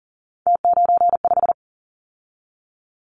Morse code, heard as a keyed tone, about 38 wpm, 700 Hz. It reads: T95
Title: Callsign Prefixes for Top 100 DXCC - 40wpm